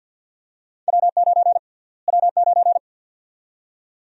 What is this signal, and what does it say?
Morse code, heard as a keyed tone, about 46 wpm, 700 Hz. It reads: W9 W9
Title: USA Call Sign Prefixes - Repeated 2x - 50wpm